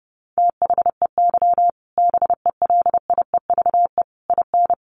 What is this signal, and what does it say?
Morse code, heard as a keyed tone, about 28 wpm, 700 Hz. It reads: THEY BELIEVE IN
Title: Sentences from Top 1000 Words 30wpm